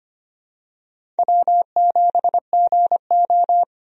Morse code, heard as a keyed tone, about 25 wpm, 700 Hz. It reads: W7GO
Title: USA Call Signs - 4 Characters - 25wpm